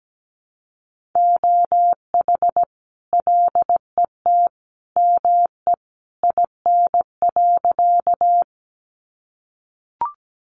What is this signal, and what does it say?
Morse code, heard as a keyed tone, about 17 wpm, 700 Hz. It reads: OH LET ME IN.
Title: Sentences from Top 500 Words 17wpm